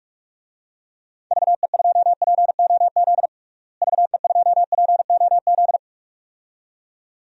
Morse code, heard as a keyed tone, about 42 wpm, 700 Hz. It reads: VE2POZ VE2POZ
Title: DXCC-100 Call Signs - 6 Characters - Repeated 2x - 45wpm